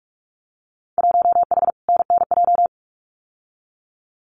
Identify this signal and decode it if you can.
Morse code, heard as a keyed tone, about 46 wpm, 700 Hz. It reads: 1H NNJ
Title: Winter Field Day Exchanges Encore 45wpm